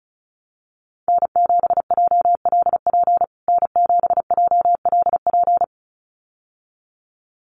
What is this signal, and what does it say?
Morse code, heard as a keyed tone, about 34 wpm, 700 Hz. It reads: N7JLP N7JLP
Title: USA Call Signs Repeated 2x 35wpm